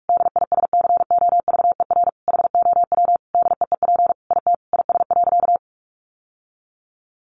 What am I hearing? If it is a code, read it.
Morse code, heard as a keyed tone, about 46 wpm, 700 Hz. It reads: DISCOVER HOW DEEP IT IS.
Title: Sentences from Top 1100 Words 45wpm